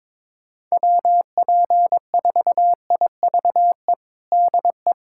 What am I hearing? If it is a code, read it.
Morse code, heard as a keyed tone, about 22 wpm, 700 Hz. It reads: WP4IVE DE
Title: USA Callsign DE Callsign 22wpm